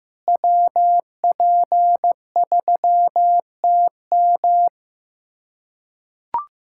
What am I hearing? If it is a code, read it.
Morse code, heard as a keyed tone, about 15 wpm, 700 Hz. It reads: WP3TM
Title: USA Callsign DE Callsign - 5 Characters - 15wpm